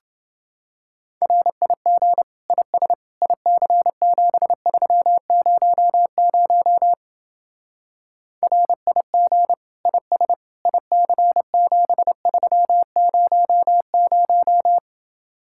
Morse code, heard as a keyed tone, about 29 wpm, 700 Hz. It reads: RIG IS IC7300 RIG IS IC7300